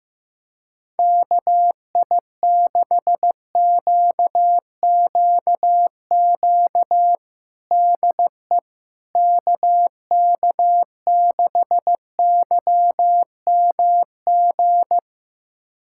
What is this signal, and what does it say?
Morse code, heard as a keyed tone, about 15 wpm, 700 Hz. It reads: KI6QQQ DE KK6YMG
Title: USA Callsign DE Callsign 15wpm